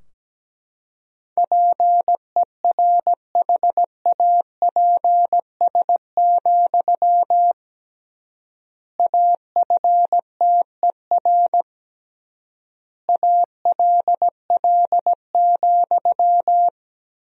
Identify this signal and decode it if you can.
Morse code, heard as a keyed tone, about 17 wpm, 700 Hz. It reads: PERHAPS, AFTER ALL,
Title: 3x Word-Spacing - Sentences from Top 500 Words - 17wpm